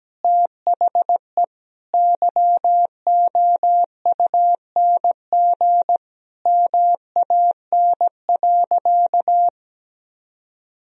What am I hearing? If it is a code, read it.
Morse code, heard as a keyed tone, about 17 wpm, 700 Hz. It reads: THE YOUNG MAN.
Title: Sentences from Top 400 Words and Abbreviations 17wpm